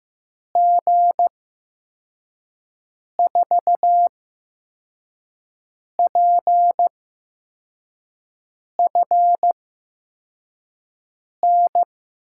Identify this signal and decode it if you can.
Morse code, heard as a keyed tone, about 15 wpm, spaced as for 4 wpm, 700 Hz. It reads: G4PFN
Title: USA Call Signs - 6 Characters - 8x Character Spacing - 15wpm